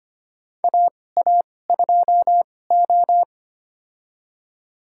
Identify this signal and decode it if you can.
Morse code, heard as a keyed tone, about 25 wpm, spaced as for 18 wpm, 700 Hz. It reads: AA2O
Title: USA Call Signs - 4 Characters - 2x Character Spacing - 25wpm